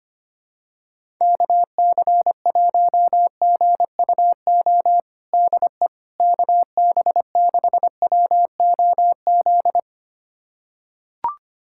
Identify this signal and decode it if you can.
Morse code, heard as a keyed tone, about 25 wpm, 700 Hz. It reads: KC1GUO DE KB6WOZ
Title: USA Callsign DE Callsign 25wpm